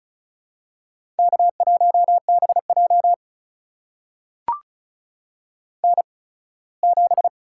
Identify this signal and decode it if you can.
Morse code, heard as a keyed tone, about 36 wpm, spaced as for 21 wpm, 700 Hz. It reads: K1BJ N7
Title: USA Call Signs - 4 Characters - 8x Character Spacing - 35wpm